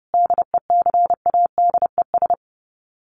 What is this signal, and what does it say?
Morse code, heard as a keyed tone, about 30 wpm, 700 Hz. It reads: DECADES
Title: Sets of 5 Words Encore2 30wpm